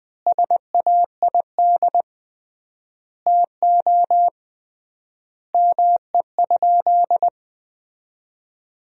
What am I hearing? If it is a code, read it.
Morse code, heard as a keyed tone, about 20 wpm, 700 Hz. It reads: SAID TO ME?
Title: 3x Word-Spacing - Sentences from Top 200 Words - 20wpm